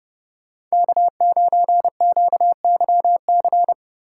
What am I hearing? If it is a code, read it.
Morse code, heard as a keyed tone, about 38 wpm, 700 Hz. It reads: K9QYC